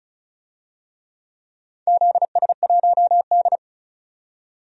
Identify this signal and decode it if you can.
Morse code, heard as a keyed tone, about 36 wpm, 700 Hz. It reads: ZS1D
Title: DXCC-100 Call Signs - 4 Characters - 35wpm